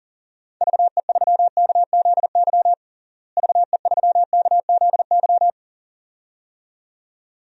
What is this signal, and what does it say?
Morse code, heard as a keyed tone, about 37 wpm, 700 Hz. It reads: VE3KZY VE3KZY
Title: DXCC-100 Call Signs - 6 Characters - Repeated 2x - 40wpm